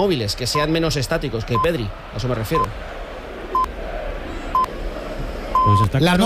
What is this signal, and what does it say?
Radio time signal (time pips), with speech. It is heard from 0.5 to 5.8 s.